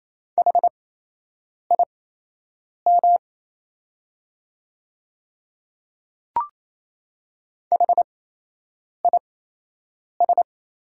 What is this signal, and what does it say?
Morse code, heard as a keyed tone, about 28 wpm, spaced as for 8 wpm, 700 Hz. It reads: HIM HIS
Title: 8x Character Spacing - Top 100 Words - 28wpm